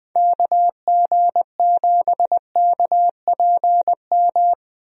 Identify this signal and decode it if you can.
Morse code, heard as a keyed tone, about 21 wpm, 700 Hz. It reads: KG7KPM
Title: USA Call Signs - 6 Characters - 20wpm